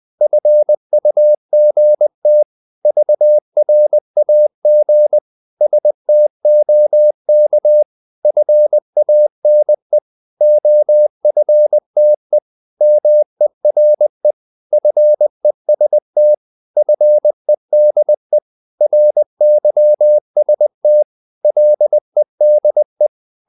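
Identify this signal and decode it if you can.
Morse code, heard as a keyed tone, about 20 wpm, 600 Hz. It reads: FUGT VRAG STOK FANE OFTE MERE FEST FEDE RYST LEDE
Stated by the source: Morse code practice of short Danish words at 20 wpm, from CW med Gnister.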